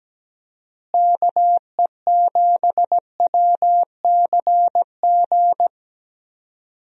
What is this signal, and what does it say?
Morse code, heard as a keyed tone, about 17 wpm, 700 Hz. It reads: KE7WCG
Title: USA Call Signs - 6 Characters - Repeated 3x - 17wpm